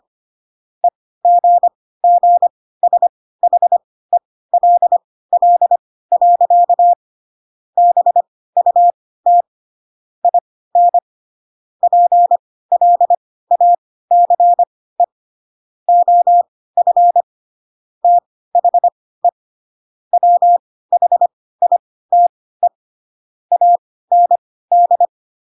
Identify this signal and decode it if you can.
Morse code, heard as a keyed tone, about 25 wpm, spaced as for 16 wpm, 700 Hz. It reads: EGGSHELL. BUT IN PLACE OF THE WHITE AND
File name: Pinocchio / Pinocchio ch5 10 25 ik1hge